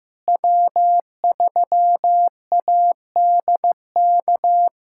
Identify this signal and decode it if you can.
Morse code, heard as a keyed tone, about 15 wpm, 700 Hz. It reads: W3ADK
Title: USA Callsign DE Callsign - 5 Characters - 15wpm